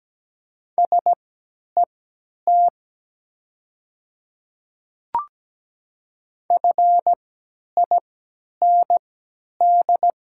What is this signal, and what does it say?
Morse code, heard as a keyed tone, about 17 wpm, spaced as for 10 wpm, 700 Hz. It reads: SET FIND
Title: 3x Character Spacing - Top 200 Words - 17wpm